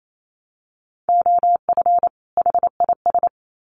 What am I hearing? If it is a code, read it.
Morse code, heard as a keyed tone, about 28 wpm, 700 Hz. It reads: OF HIS
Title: Sets of 2 Words 28wpm